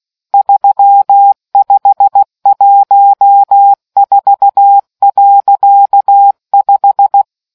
Morse code, heard as a keyed tone, about 16 wpm, 800 Hz. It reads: 3514.5